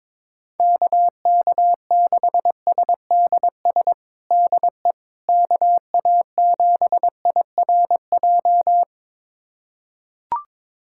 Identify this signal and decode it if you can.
Morse code, heard as a keyed tone, about 22 wpm, 700 Hz. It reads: KK6SDS DE KA7IRJ
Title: USA Callsign DE Callsign - 6 Characters - 22wpm